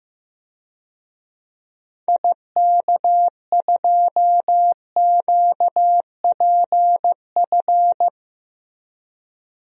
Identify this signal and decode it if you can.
Morse code, heard as a keyed tone, about 15 wpm, 700 Hz. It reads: IK2QPF
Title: DXCC-100 Call Signs - 6 Characters - Repeated 2x - 15wpm